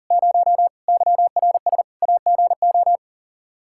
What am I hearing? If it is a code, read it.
Morse code, heard as a keyed tone, about 40 wpm, 700 Hz. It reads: 0 YRS AGO
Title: Practice QSO - Exchange 3 - More Info - 40wpm